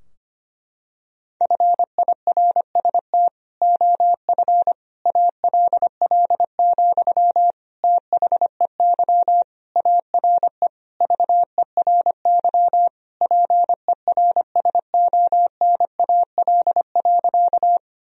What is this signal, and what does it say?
Morse code, heard as a keyed tone, about 25 wpm, 700 Hz. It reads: FIRST OF ALL, THEY ARE VERY PERSONAL.
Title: Sentences from Top 1000 Words - Review Part 4 - 25wpm